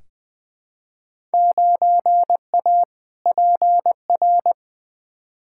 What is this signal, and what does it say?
Morse code, heard as a keyed tone, about 20 wpm, 700 Hz. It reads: 9A PR
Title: Field Day Exchanges Encore - 20wpm